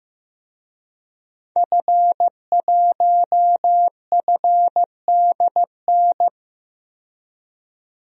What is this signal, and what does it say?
Morse code, heard as a keyed tone, about 15 wpm, 700 Hz. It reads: F1FDN
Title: DXCC-100 Call Signs - 5 Characters - 15wpm